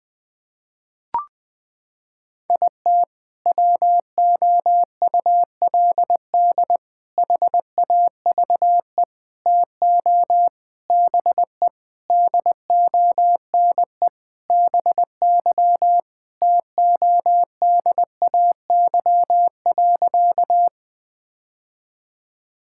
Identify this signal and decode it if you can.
Morse code, heard as a keyed tone, about 20 wpm, 700 Hz. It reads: IT WOULD HAVE TO BE DONE BY TODAY.
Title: Sentences with 4-Word Phrases - 20wpm